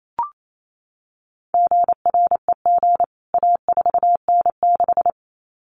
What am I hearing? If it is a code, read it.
Morse code, heard as a keyed tone, about 28 wpm, 700 Hz. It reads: GREG A4N6
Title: CWT Exchange 28wpm